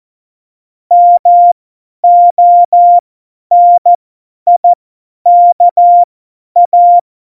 Morse code, heard as a keyed tone, about 14 wpm, spaced as for 10 wpm, 700 Hz. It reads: MONIKA
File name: Morsekurs / Witze / Witz 05 Mai 2020, 50 F70